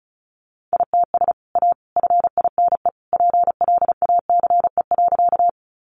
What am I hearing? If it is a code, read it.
Morse code, heard as a keyed tone, about 34 wpm, 700 Hz. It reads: ITS A FINE PLACE.
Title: Sentences from Top 600 Words 35wpm